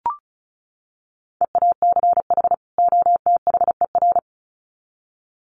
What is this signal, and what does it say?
Morse code, heard as a keyed tone, about 34 wpm, 700 Hz. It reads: EACH OTHER
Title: Sets of 2 Words 35wpm